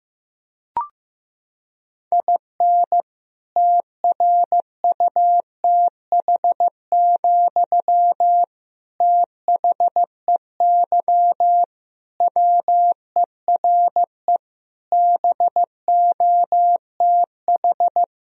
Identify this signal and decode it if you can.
Morse code, heard as a keyed tone, about 15 wpm, 700 Hz. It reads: IN TRUTH, THEY WERE BOTH
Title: Sentences from Top 1000 Words - Review Part 1 - 15wpm